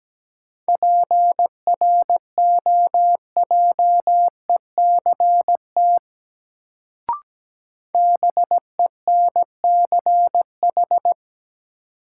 Morse code, heard as a keyed tone, about 17 wpm, 700 Hz. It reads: PROJECT BENCH
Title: Sentences from Top 1400 Words - New Word Drill Alt - 17wpm